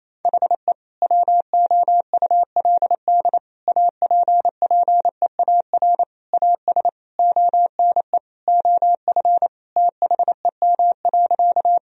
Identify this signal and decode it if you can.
Morse code, heard as a keyed tone, about 28 wpm, 700 Hz. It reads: HE WOULD APPEAR AS ONE OF THEM.
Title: Sentences from Top 1000 Words - Review Part 2 - 28wpm